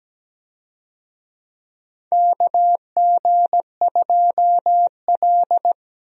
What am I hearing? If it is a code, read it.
Morse code, heard as a keyed tone, about 17 wpm, 700 Hz. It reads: KG2L
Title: USA Call Signs - 4 Characters - Repeated 2x - 17wpm